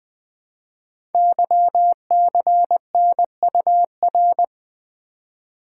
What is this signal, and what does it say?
Morse code, heard as a keyed tone, about 20 wpm, 700 Hz. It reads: YCNUR
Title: Code Groups - 5 Letter:Numbers - 20wpm